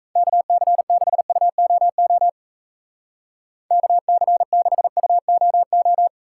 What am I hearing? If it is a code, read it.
Morse code, heard as a keyed tone, about 34 wpm, 700 Hz. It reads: KC6UOO KC6UOO
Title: Speed Racing - USA 2x3 Call Signs 28wpm